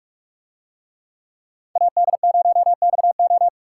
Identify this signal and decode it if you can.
Morse code, heard as a keyed tone, about 44 wpm, 700 Hz. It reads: AD0XO